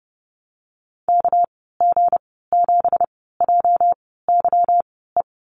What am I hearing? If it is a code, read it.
Morse code, heard as a keyed tone, about 30 wpm, spaced as for 17 wpm, 700 Hz. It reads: KG7JYE
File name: Lesson 37g - Letter x - Call Signs 30wpm